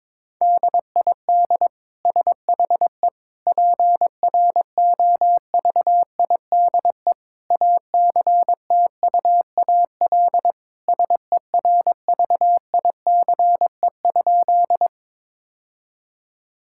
Morse code, heard as a keyed tone, about 22 wpm, 700 Hz. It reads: DID SHE PROVIDE ACTUAL SERVICE?
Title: Sentences from Top 1000 Words - Review Part 5 - 22wpm